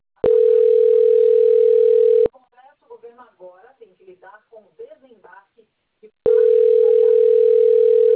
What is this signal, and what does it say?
Ringback tone (North America), 440 + 480 Hz, with speech.
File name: rg-1600-unknown- - - .5953